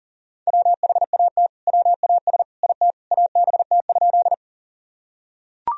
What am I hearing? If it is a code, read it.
Morse code, heard as a keyed tone, about 40 wpm, 700 Hz. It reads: WHAT WAS IT ABT?
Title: Sentences from Top 100 Words and Abbreviations 40wpm